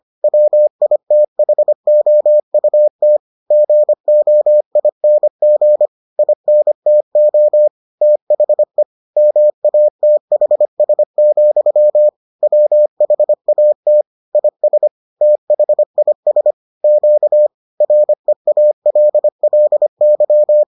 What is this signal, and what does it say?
Morse code, heard as a keyed tone, about 25 wpm, 600 Hz. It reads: WITHOUT GOING INTO THE MATHS, WHAT IS THIS Q REALLY